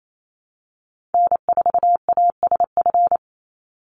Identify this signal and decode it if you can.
Morse code, heard as a keyed tone, about 28 wpm, 700 Hz. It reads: N4ASF
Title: Call Signs - 28wpm